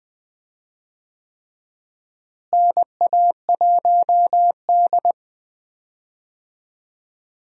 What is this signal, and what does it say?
Morse code, heard as a keyed tone, about 20 wpm, 700 Hz. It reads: NA1D
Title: USA Call Signs - 4 Characters - Repeated 3x - 20wpm